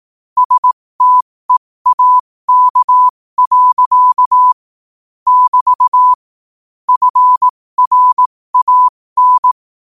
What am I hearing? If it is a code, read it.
Morse code, heard as a keyed tone, about 18 wpm, spaced as for 15 wpm, 1000 Hz. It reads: STEAK. <BT> FRAN
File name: qotdfiles / 15wpm